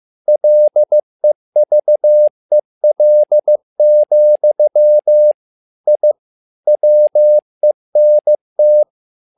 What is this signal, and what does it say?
Morse code, heard as a keyed tone, about 15 wpm, 600 Hz. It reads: LEVEL, I WENT
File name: learning center / tower trouble / tower trouble 15wpm